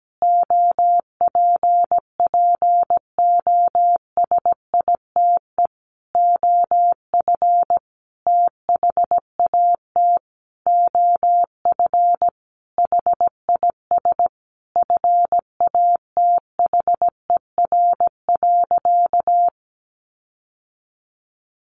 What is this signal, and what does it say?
Morse code, heard as a keyed tone, about 17 wpm, 700 Hz. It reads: OPPOSITE OF THAT OF HIS FATHER.
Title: Sentences from Top 1300 Words 17wpm